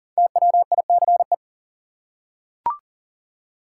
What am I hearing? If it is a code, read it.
Morse code, heard as a keyed tone, about 40 wpm, 700 Hz. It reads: TWICE
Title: Five Letter Words - 40wpm